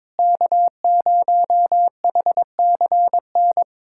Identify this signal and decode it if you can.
Morse code, heard as a keyed tone, about 22 wpm, 700 Hz. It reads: K0HCN
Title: USA Callsign DE Callsign - 5 Characters - 22wpm